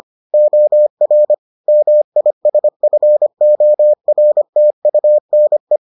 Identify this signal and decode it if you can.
Morse code, heard as a keyed tone, about 25 wpm, 600 Hz. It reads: OR MISFORTUNE